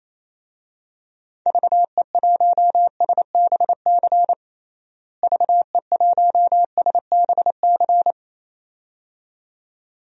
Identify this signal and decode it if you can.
Morse code, heard as a keyed tone, about 27 wpm, 700 Hz. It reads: VE1SBC VE1SBC
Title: DXCC-100 Call Signs - 6 Characters - Repeated 2x - 28wpm